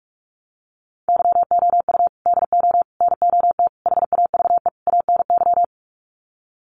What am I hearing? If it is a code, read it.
Morse code, heard as a keyed tone, about 44 wpm, 700 Hz. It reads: YOU DO NOT HAVE ANY
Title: Sets of 5 Words Encore 45wpm